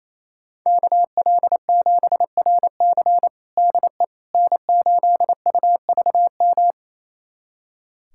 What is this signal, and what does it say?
Morse code, heard as a keyed tone, about 28 wpm, 700 Hz. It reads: KL7RC DE N8UVM
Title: USA Callsign DE Callsign - 5 Characters - 28wpm